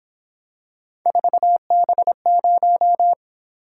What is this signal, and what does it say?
Morse code, heard as a keyed tone, about 26 wpm, 700 Hz. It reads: 4B0
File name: Speed-Racing - Three Letter-Number Combinations - 22wpm